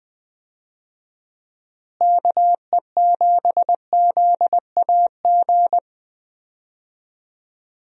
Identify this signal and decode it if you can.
Morse code, heard as a keyed tone, about 20 wpm, 700 Hz. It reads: KE7ZAG
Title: USA Call Signs - 6 Characters - Repeated 3x - 20wpm